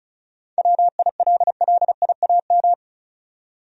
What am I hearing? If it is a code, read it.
Morse code, heard as a keyed tone, about 36 wpm, 700 Hz. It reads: WILLIAM
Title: USA Common Names - 35wpm